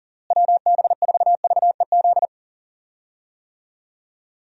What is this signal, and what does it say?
Morse code, heard as a keyed tone, about 39 wpm, 700 Hz. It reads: WB4VEZ
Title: USA Call Signs - 6 Characters - Repeated 2x - 40wpm